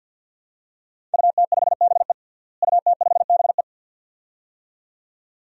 Morse code, heard as a keyed tone, about 53 wpm, 700 Hz. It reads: UT5BE UT5BE
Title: DXCC-100 Call Signs - 5 Characters - Repeated 2x - 50wpm